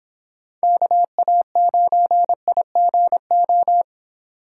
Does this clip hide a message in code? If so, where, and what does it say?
Yes: Morse code from 0.6 to 3.8 s, heard as a keyed tone, about 26 wpm, 700 Hz. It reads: KA9IGO